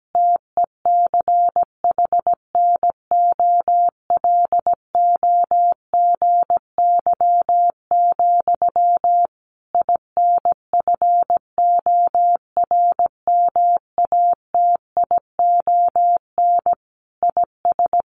Morse code, heard as a keyed tone, about 17 wpm, 700 Hz. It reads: TECHNOLOGY, INFORMATION IS
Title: Sentences from Top 1200 Words 17wpm